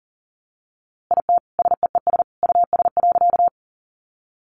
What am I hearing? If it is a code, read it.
Morse code, heard as a keyed tone, about 40 wpm, 700 Hz. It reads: IT SEES US.
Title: A Sound Of Thunder 40wpm